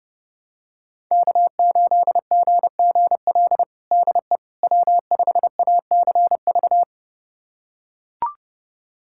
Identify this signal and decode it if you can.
Morse code, heard as a keyed tone, about 32 wpm, 700 Hz. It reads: K8GGL DE W5ACV
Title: USA Callsign DE Callsign - 5 Characters - 30wpm